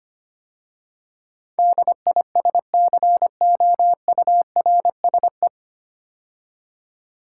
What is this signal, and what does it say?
Morse code, heard as a keyed tone, about 25 wpm, 700 Hz. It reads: DISCOURSE
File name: practice / 25wpm / 9 Letter Words - First 3 Letters Given - 25wpm